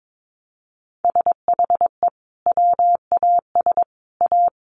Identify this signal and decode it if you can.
Morse code, heard as a keyed tone, about 22 wpm, 700 Hz. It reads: SHE WAS A
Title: Sets of 3 Words 22wpm